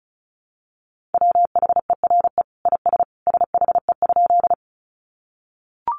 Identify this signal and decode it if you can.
Morse code, heard as a keyed tone, about 34 wpm, 700 Hz. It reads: WHERE IS SHE?
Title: Sentences from Top 200 Words 35wpm